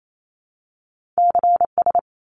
Morse code, heard as a keyed tone, about 28 wpm, 700 Hz. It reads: CS
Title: Abbreviations 28wpm